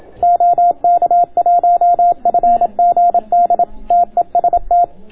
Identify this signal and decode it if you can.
Morse code, heard as a keyed tone, about 27 wpm, 690 Hz. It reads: OK1FGD TEST